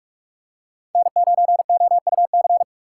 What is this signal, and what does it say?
Morse code, heard as a keyed tone, about 46 wpm, 700 Hz. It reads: N9OUC
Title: USA Call Signs - 6x Character Spacing - 45wpm